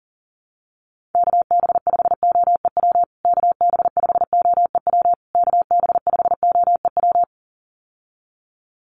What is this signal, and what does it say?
Morse code, heard as a keyed tone, about 40 wpm, 700 Hz. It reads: KB5OEW KB5OEW KB5OEW
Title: USA Call Signs Repeated 3x 40wpm